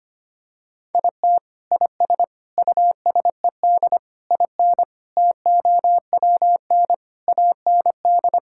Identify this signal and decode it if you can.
Morse code, heard as a keyed tone, about 25 wpm, 700 Hz. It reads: IT IS USED IN TOWN AND
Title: Sentences from Top 1600 Words 25wpm